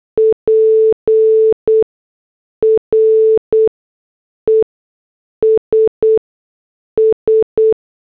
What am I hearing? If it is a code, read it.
Morse code, heard as a keyed tone, about 8 wpm, spaced as for 6 wpm, 440 Hz. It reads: PRESS
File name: morse / headlines